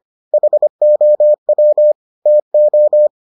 Morse code, heard as a keyed tone, about 25 wpm, 600 Hz. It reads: HOW TO